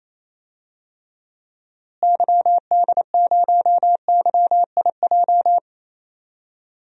Morse code, heard as a keyed tone, about 28 wpm, 700 Hz. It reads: YD0YIJ
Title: DXCC-100 Call Signs - 6 Characters - Repeated 3x - 28wpm